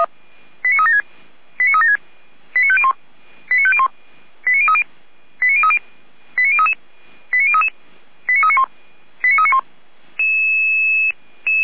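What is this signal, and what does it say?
Five-tone selective call (ZVEI), heard as quick runs of five tones, tones of about 70 ms. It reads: 89376 89376 89501 89501 89030 89030 89033 89033 89391 89391
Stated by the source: ZVEI five-tone call received by SDR.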